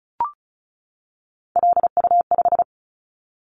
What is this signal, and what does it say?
Morse code, heard as a keyed tone, about 36 wpm, 690 Hz. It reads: LU5
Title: Callsign Prefixes for Top 50 DXCC - 35wpm